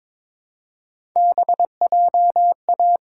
Morse code, heard as a keyed tone, about 22 wpm, 700 Hz. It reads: BJA
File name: practice / 22wpm / Three Letter-Number Combinations - 22wpm